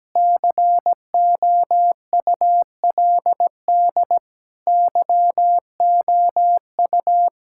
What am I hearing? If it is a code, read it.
Morse code, heard as a keyed tone, about 17 wpm, 700 Hz. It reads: COULD YOU
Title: Sentences from Top 1600 Words 17wpm